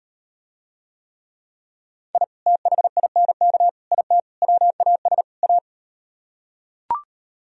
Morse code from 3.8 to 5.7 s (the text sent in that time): IT WAS A